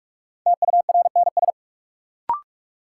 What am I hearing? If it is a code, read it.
Morse code, heard as a keyed tone, about 41 wpm, 700 Hz. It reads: TURNS
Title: Sets of 5 Words Encore - New Word Drill - 45wpm